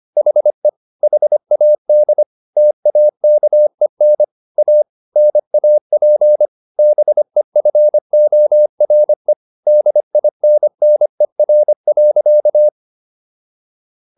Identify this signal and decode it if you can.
Morse code, heard as a keyed tone, about 25 wpm, 600 Hz. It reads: HE HAD TAKEN A NAP BEFORE DINNER.